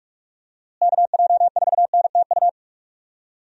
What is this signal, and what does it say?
Morse code, heard as a keyed tone, about 46 wpm, 700 Hz. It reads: KJ4NTU
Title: USA Call Signs - 6 Characters - Repeated 3x - 45wpm